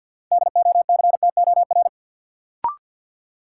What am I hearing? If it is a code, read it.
Morse code, heard as a keyed tone, about 46 wpm, 700 Hz. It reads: DOCTOR
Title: Sentences from Top 1500 Words - New Word Drill Alt - 50wpm